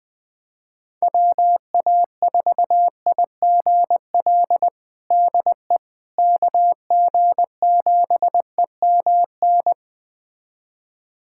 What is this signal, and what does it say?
Morse code, heard as a keyed tone, about 20 wpm, 700 Hz. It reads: WA4IGL DE KG7EMN
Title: USA Callsign DE Callsign - 6 Characters - 20wpm